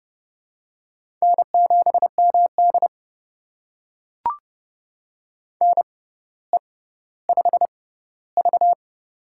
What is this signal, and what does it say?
Morse code, heard as a keyed tone, about 30 wpm, spaced as for 15 wpm, 700 Hz. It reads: N7MD NE5V
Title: USA Call Signs - 4 Characters - 6x Character Spacing - 30wpm